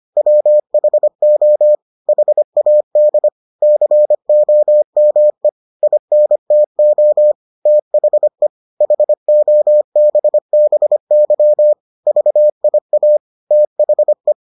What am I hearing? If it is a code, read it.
Morse code, heard as a keyed tone, about 25 wpm, 600 Hz. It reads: WHO HAD COME INTO THE HOBBY VIA THE